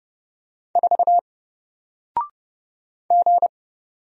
Morse code, heard as a keyed tone, about 30 wpm, 700 Hz. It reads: 4 G
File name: Single-Letter-Number-30wpm